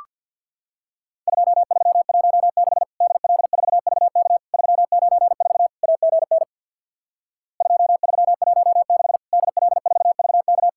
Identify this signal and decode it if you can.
Morse code, heard as a keyed tone, about 50 wpm, 700 Hz. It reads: 2316 DL4VK 394 AGN 2316 DL4VK
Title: Worked All Europe DX Contest - QTC Exchange with Repeats - 50wpm